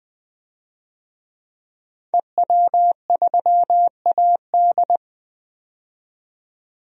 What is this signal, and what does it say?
Morse code, heard as a keyed tone, about 20 wpm, 700 Hz. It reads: EW3AD
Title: DXCC-100 Call Signs - 5 Characters - 20wpm